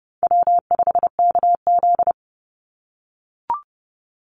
Morse code, heard as a keyed tone, about 30 wpm, 700 Hz. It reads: W5KZ
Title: USA Call Signs 30wpm